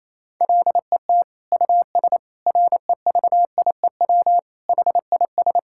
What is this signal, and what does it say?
Morse code, heard as a keyed tone, about 28 wpm, 700 Hz. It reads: LET US REVIEW HIS
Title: Sentences from Top 1000 Words - Review Part 6 - 28wpm